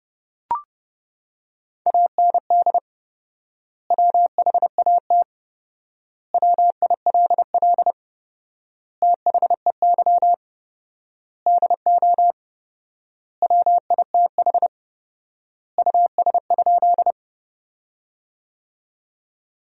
Morse code, heard as a keyed tone, about 30 wpm, 700 Hz. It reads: AND WHAT WILL THEY DO WITH US?
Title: 4x Word-Spacing - Sentences from Top 200 Words - 30wpm